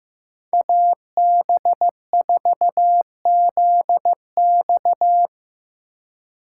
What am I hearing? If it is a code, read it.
Morse code, heard as a keyed tone, about 15 wpm, 700 Hz. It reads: AB4ZX
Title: USA Call Signs - 5 Characters - 4x Character Spacing - 15wpm